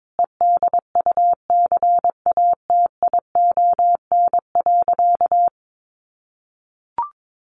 Morse code, heard as a keyed tone, about 22 wpm, 700 Hz. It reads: EDUCATION.
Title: Sentences from Top 1500 Words 22wpm